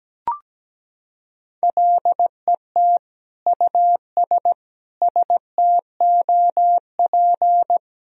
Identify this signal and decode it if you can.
Morse code, heard as a keyed tone, about 17 wpm, 700 Hz. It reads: LET US STOP
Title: Sentences from Top 400 Words and Abbreviations 17wpm